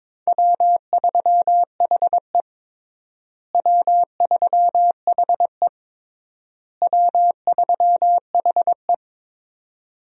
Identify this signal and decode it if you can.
Morse code, heard as a keyed tone, about 21 wpm, 700 Hz. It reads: W3HE W3HE W3HE
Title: USA Call Signs - 4 Characters - Repeated 3x - 22wpm